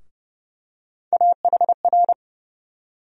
Morse code, heard as a keyed tone, about 30 wpm, 700 Hz. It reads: AHR
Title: USA Call Sign Postfixes - 3 Letters 30wpm